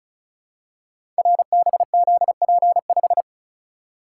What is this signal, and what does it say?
Morse code, heard as a keyed tone, about 36 wpm, 700 Hz. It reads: RBZP5
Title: Code Groups - 5 Letter:Numbers - 35wpm